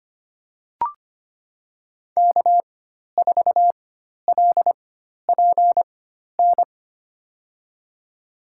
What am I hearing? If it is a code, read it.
Morse code, heard as a keyed tone, about 25 wpm, spaced as for 12 wpm, 700 Hz. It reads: K4LPN
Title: USA Call Signs - 5 Characters - 4x Character Spacing - 25wpm